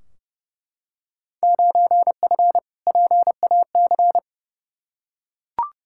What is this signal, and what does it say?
Morse code, heard as a keyed tone, about 30 wpm, 700 Hz. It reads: 9F PAC
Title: Field Day Exchanges Encore - 30wpm